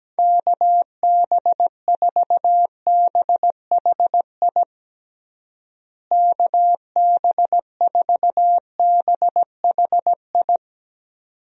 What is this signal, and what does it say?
Morse code, heard as a keyed tone, about 17 wpm, 700 Hz. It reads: KB4BHI KB4BHI
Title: USA Call Signs - 6 Characters - Repeated 2x - 17wpm